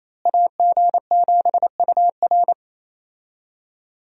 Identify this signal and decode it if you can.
Morse code, heard as a keyed tone, about 27 wpm, 700 Hz. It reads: AG7UR
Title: USA Call Signs - 5 Characters - Repeated 2x - 28wpm